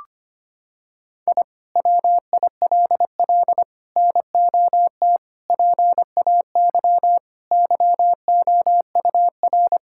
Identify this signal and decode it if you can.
Morse code, heard as a keyed tone, about 25 wpm, 700 Hz. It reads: I WILL NOT PAY YOUR
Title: Sentences from Top 1000 Words - Review Part 5 - 25wpm